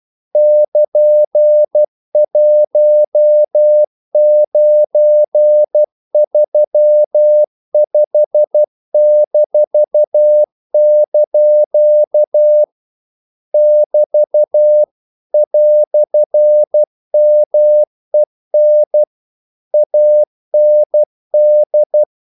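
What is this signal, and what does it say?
Morse code, heard as a keyed tone, about 12 wpm, 600 Hz. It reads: <KN>1935-) <BT> "MEN AND
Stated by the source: Morse practice sending English quotations.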